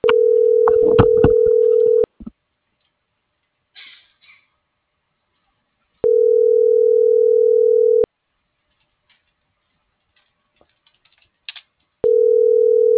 Ringback tone (North America), 440 + 480 Hz.